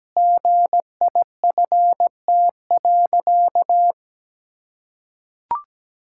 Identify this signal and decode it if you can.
Morse code, heard as a keyed tone, about 17 wpm, 700 Hz. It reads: GIFT.
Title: Sentences from Top 1000 Words - Review Part 6 - 17wpm